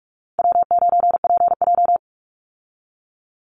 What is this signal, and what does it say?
Morse code, heard as a keyed tone, about 44 wpm, 700 Hz. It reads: W9PJ
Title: Easy Call Signs - 45wpm